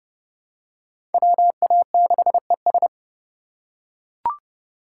Morse code, heard as a keyed tone, about 30 wpm, 700 Hz. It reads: WA6ES
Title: USA Call Signs - 5 Characters - 3x Character Spacing - 30wpm